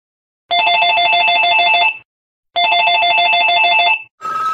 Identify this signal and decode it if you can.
There is no code.